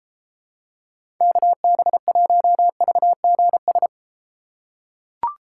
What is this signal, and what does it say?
Morse code, heard as a keyed tone, about 31 wpm, 700 Hz. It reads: KB1VGS